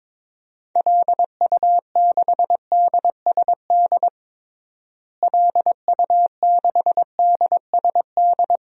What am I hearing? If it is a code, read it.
Morse code, heard as a keyed tone, about 22 wpm, 700 Hz. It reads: LU6DSD LU6DSD
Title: DXCC-100 Call Signs - 6 Characters - Repeated 3x - 22wpm